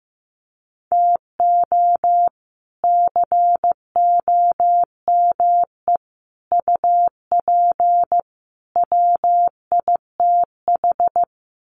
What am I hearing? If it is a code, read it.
Morse code, heard as a keyed tone, about 15 wpm, 700 Hz. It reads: TO COME UP WITH
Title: Sets of 4 Words 15wpm